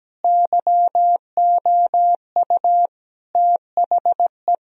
Morse code, heard as a keyed tone, about 17 wpm, 700 Hz. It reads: YOU THE
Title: Sentences from Top 1000 Words - Review Part 2 - 17wpm